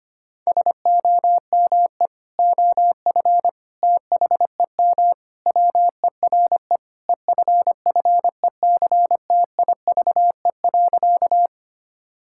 Morse code, heard as a keyed tone, about 25 wpm, 700 Hz. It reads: SOME OF THEM WERE EFFECTIVE.
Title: Sentences from Top 1000 Words - Review Part 6 - 25wpm